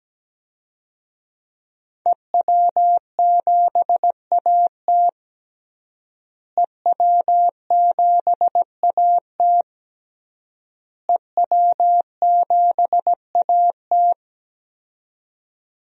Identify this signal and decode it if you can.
Morse code, heard as a keyed tone, about 17 wpm, 700 Hz. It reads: EW7AT EW7AT EW7AT
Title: DXCC-100 Call Signs - 5 Characters - Repeated 3x - 17wpm